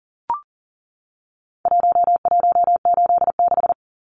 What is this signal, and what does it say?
Morse code, heard as a keyed tone, about 40 wpm, 700 Hz. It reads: 1186
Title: Numbers - Contest Style - 4-Digit 40wpm